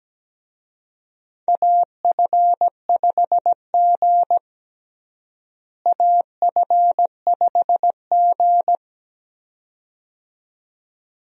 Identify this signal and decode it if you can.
Morse code, heard as a keyed tone, about 17 wpm, 700 Hz. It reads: AF5G AF5G
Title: USA Call Signs - 4 Characters - Repeated 2x - 17wpm